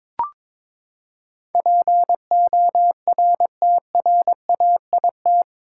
Morse code, heard as a keyed tone, about 22 wpm, 700 Hz. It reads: PORTRAIT
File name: practice / 22wpm / Eight Letter Words - 22wpm